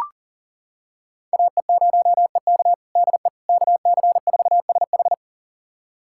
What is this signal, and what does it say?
Morse code, heard as a keyed tone, about 40 wpm, 700 Hz. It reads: AE0EK DE KC4SH
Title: USA Callsign DE Callsign - 5 Characters - 40wpm